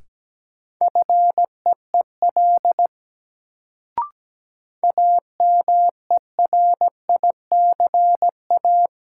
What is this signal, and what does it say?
Morse code, heard as a keyed tone, about 17 wpm, 700 Hz. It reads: FEEL AMERICA
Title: Sets of 5 Words - New Word Drill - 17wpm